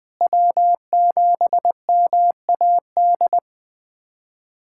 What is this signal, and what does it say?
Morse code, heard as a keyed tone, about 20 wpm, 700 Hz. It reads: W7MAD
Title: USA Callsign DE Callsign - 5 Characters - 20wpm